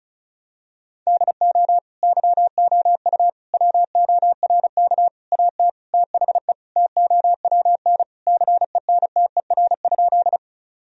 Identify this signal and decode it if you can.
Morse code, heard as a keyed tone, about 34 wpm, 700 Hz. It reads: DO YOU WORK AT THE TOWN CENTER?
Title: Sentences from Top 1000 Words - Review Part 2 - 35wpm